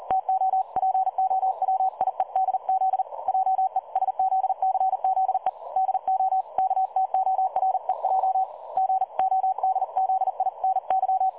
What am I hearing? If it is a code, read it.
Morse code, heard as a keyed tone, about 40 wpm, 760 Hz. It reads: TO POWIEDZ JESZCZE DO KTOREJ GODZINY
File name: dlazaawansowanych / nagraniazpasma / websdr recording 3565.0kHz